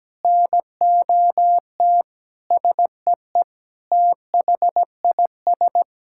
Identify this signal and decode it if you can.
Morse code, heard as a keyed tone, about 17 wpm, 700 Hz. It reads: NOT SEE THIS
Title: Sentences from Top 1600 Words 17wpm